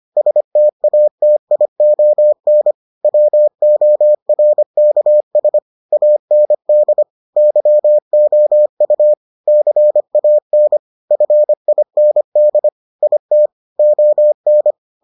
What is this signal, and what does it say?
Morse code, heard as a keyed tone, about 25 wpm, 600 Hz. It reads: STATION WORKS AND YOU CAN FIND IT ON